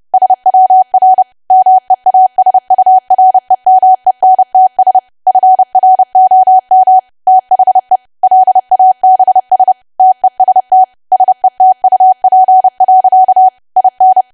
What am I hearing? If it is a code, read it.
Morse code, heard as a keyed tone, about 30 wpm, 750 Hz. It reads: SWR MEASUREMENTS FROM THE LABS TEST SETUP. IN
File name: cwa / W1AW-Level3 / 30WPM